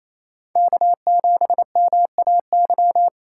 Morse code, heard as a keyed tone, about 37 wpm, 700 Hz. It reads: K7MAY